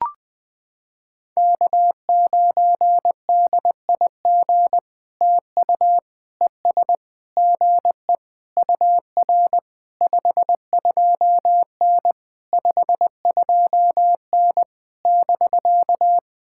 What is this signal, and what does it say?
Morse code, heard as a keyed tone, about 20 wpm, 700 Hz. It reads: K9DIG TU ES GE UR 52N 52N <BK>
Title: POTA QSO Practice - 20wpm